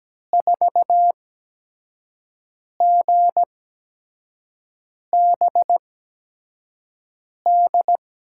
Morse code, heard as a keyed tone, about 17 wpm, spaced as for 5 wpm, 700 Hz. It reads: 4GBD
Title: USA Call Signs - 6 Characters - 8x Character Spacing - 17wpm